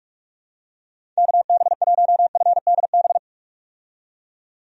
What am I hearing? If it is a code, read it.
Morse code, heard as a keyed tone, about 46 wpm, 700 Hz. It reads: KB1FDB
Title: USA Call Signs - 6 Characters - 45wpm